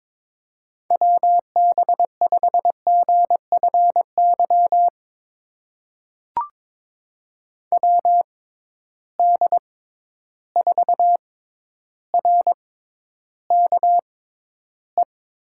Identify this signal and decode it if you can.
Morse code, heard as a keyed tone, about 22 wpm, spaced as for 11 wpm, 700 Hz. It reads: WB5GFY WD4RKE
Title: USA Call Signs - 6 Characters - 6x Character Spacing - 22wpm